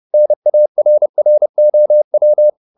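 Morse code, heard as a keyed tone, about 30 wpm, 600 Hz. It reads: NARROW
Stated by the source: Morse code at 30 wpm, the novel Never by Ken Follett in English, from CW med Gnister.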